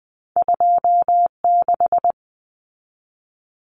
Morse code, heard as a keyed tone, about 20 wpm, 700 Hz. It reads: 26
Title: Common Cut Numbers - 2-Digit 20wpm